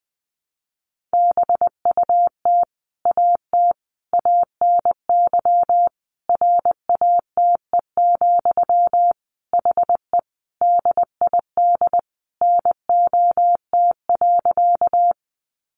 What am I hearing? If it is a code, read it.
Morse code, heard as a keyed tone, about 20 wpm, 700 Hz. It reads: BUT AT ANY RATE, HE DID NOT.
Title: Sentences from Top 700 Words 20wpm